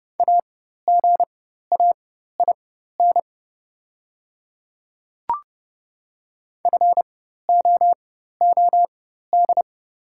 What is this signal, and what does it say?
Morse code, heard as a keyed tone, about 30 wpm, spaced as for 14 wpm, 700 Hz. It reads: AGAIN FOOD
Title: 4x Character Spacing - Top 300 Words - 30wpm